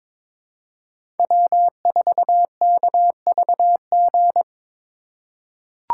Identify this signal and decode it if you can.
Morse code, heard as a keyed tone, about 21 wpm, 700 Hz. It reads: W4KVG
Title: USA Call Signs - 5 Characters - 4x Character Spacing - 22wpm